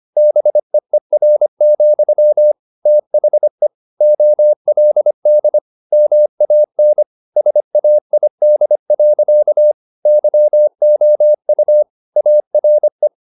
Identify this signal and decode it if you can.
Morse code, heard as a keyed tone, about 25 wpm, 600 Hz. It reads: BEER, THE OLD MAN SAID. YOU ARE
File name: the old man and the sea part 1 25wpm